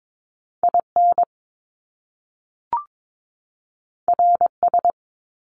Morse code, heard as a keyed tone, about 22 wpm, 700 Hz. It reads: IN RS
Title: Top Two Letter Combinations in Words 22wpm